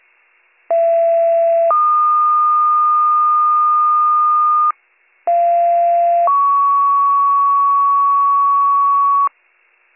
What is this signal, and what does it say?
Two-tone page, heard as a one-second tone, then a three-second tone, about 670 then 1220 Hz, about 690 then 1120 Hz. It is heard from 0.7 to 9.3 s.